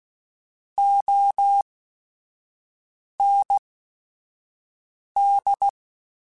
Morse code, heard as a keyed tone, about 16 wpm, spaced as for 5 wpm, 800 Hz. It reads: OND
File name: Technical Subjects / Technical Subjects-mp3s / Sw Pwr Sup @5